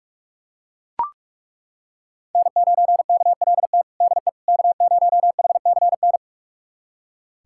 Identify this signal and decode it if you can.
Morse code, heard as a keyed tone, about 41 wpm, 700 Hz. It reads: N9KLT DE K0HCN
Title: USA Callsign DE Callsign - 5 Characters - 45wpm